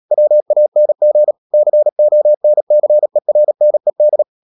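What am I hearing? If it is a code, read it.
Morse code, heard as a keyed tone, about 37 wpm, 600 Hz. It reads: WANG CONCERNED
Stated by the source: Morse code practice at 35–39 wpm sending English text of Ken Follett's novel Never, from CW med Gnister.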